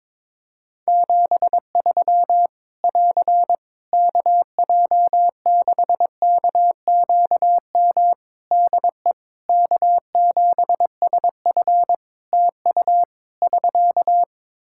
Morse code, heard as a keyed tone, about 23 wpm, 700 Hz. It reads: 73 <AR> KJ6KQM DE K7SF TU <SK>
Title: Practice QSO - Exchange 4 - Ending - 22wpm